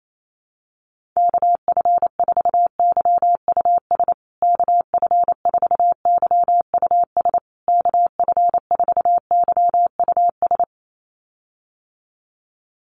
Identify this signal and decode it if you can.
Morse code, heard as a keyed tone, about 28 wpm, 700 Hz. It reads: KF4YUS KF4YUS KF4YUS
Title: USA Call Signs Repeated 3x 28wpm